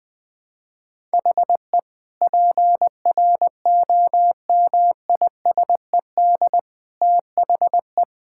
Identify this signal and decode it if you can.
Morse code, heard as a keyed tone, about 20 wpm, 700 Hz. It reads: HE PROMISED THE
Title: Sentences with 8-Word Phrases - 20wpm